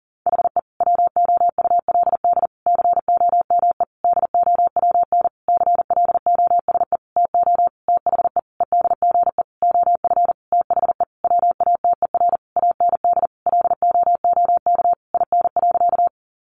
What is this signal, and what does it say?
Morse code, heard as a keyed tone, about 40 wpm, 700 Hz. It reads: HE WOULD COME DOWN CLOSE TO THE EDGE OF THE WATER AND LOOK IN.
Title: Sentences from Top 1300 Words 40wpm